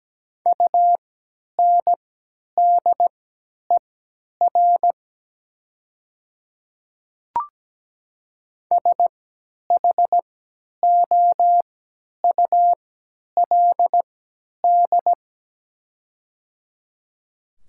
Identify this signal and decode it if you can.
Morse code, heard as a keyed tone, about 17 wpm, spaced as for 10 wpm, 700 Hz. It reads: UNDER SHOULD
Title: 3x Character Spacing - Top 300 Words - 17wpm